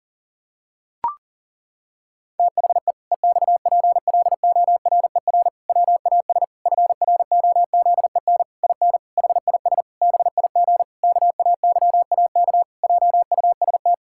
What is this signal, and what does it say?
Morse code, heard as a keyed tone, about 40 wpm, 700 Hz. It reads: THE EXPLORER WAS FROZEN IN HIS BIG KAYAK JUST
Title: Pangram Challenge v2 - 40wpm